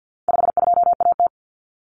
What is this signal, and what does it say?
Morse code, heard as a keyed tone, about 49 wpm, 700 Hz. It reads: 52AT
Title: Common Cut Numbers - 4-Digit 50wpm